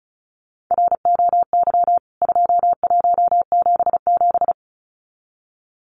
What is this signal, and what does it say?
Morse code, heard as a keyed tone, about 34 wpm, 700 Hz. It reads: ROY 2177
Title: CWT Exchange 35wpm